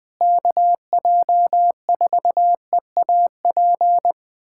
Morse code, heard as a keyed tone, about 20 wpm, 700 Hz. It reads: KJ4EAP
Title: USA Callsign DE Callsign 20wpm